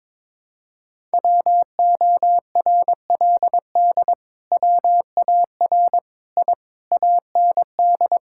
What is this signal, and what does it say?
Morse code, heard as a keyed tone, about 21 wpm, 700 Hz. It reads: WORLD WAR I AND
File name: Sentences with 7-Word Phrases - 22wpm